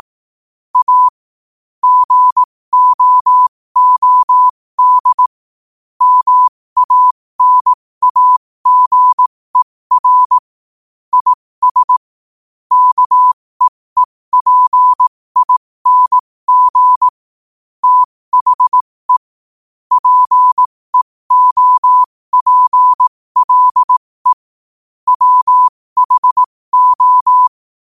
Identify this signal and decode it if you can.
Morse code, heard as a keyed tone, about 18 wpm, spaced as for 15 wpm, 1000 Hz. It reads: A GOOD MANAGER IS KEEPING THE PEOPLE WHO
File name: qotdfiles / 15wpm